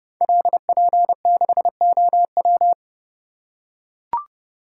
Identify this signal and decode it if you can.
Morse code, heard as a keyed tone, about 30 wpm, 700 Hz. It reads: LP6OW
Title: Code Groups - 5 Letter:Numbers - 30wpm